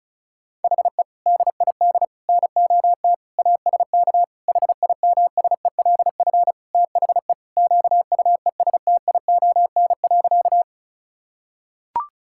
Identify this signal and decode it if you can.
Morse code, heard as a keyed tone, about 36 wpm, 700 Hz. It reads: HE DID NOT ASK HIMSELF THE QUESTION.
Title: Sentences from Top 1600 Words 35wpm